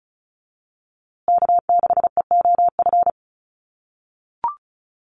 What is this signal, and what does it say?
Morse code, heard as a keyed tone, about 34 wpm, 700 Hz. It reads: K6EOF
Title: USA Call Signs Repeated 3x 35wpm